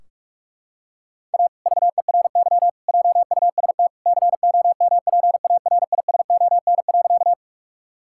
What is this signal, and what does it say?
Morse code, heard as a keyed tone, about 46 wpm, 700 Hz. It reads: A VERY JUST COMPARISON.
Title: Sentences from Top 1600 Words 45wpm